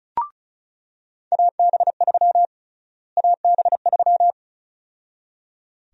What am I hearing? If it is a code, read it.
Morse code, heard as a keyed tone, about 36 wpm, 700 Hz. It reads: AB3 AB3
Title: USA Call Sign Prefixes - Repeated 2x - 35wpm